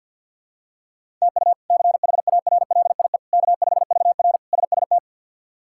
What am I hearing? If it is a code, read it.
Morse code, heard as a keyed tone, about 51 wpm, 700 Hz. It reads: TU CHARLIE K5VR SST
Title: K1USN SST QSO Practice - 50wpm